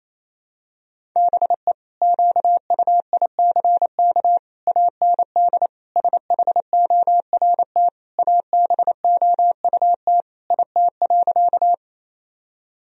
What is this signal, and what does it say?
Morse code, heard as a keyed tone, about 28 wpm, 700 Hz. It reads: BE QUICK AND SHORT ABOUT IT.
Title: Sentences from Top 1000 Words - Review Part 2 - 28wpm